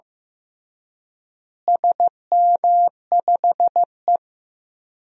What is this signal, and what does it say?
Morse code, heard as a keyed tone, about 15 wpm, 700 Hz. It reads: SM5E
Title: DXCC-100 Call Signs - 4 Characters - Repeated 2x - 15wpm